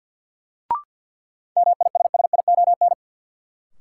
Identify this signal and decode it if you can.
Morse code, heard as a keyed tone, about 53 wpm, 700 Hz. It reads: MISSION
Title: Sentences from Top 1600 Words - New Word Drill - 50wpm